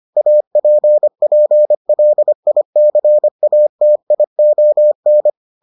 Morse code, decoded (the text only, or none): APPLICATION